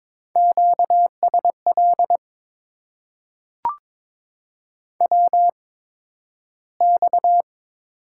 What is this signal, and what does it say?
Morse code, heard as a keyed tone, about 22 wpm, spaced as for 12 wpm, 700 Hz. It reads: QSL WX
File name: Top 90 QSO Elements - 8x Character Spacing - 22wpm